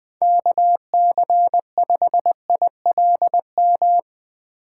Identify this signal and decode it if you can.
Morse code, heard as a keyed tone, about 20 wpm, 700 Hz. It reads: KC5ILM
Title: Speed Racing - USA 2x3 Call Signs 20wpm